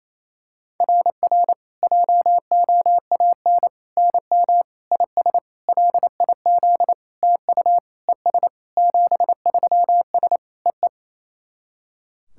Morse code, heard as a keyed tone, about 28 wpm, 690 Hz. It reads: RR JOAN NM IS LIZ TU ES 73S EE
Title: SOTA QSO Practice 28wpm